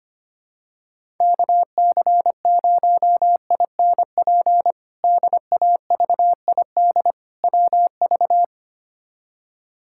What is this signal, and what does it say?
Morse code, heard as a keyed tone, about 25 wpm, 700 Hz. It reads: KC0INP DAVID WV
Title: K1USN SST QSO Practice - 25wpm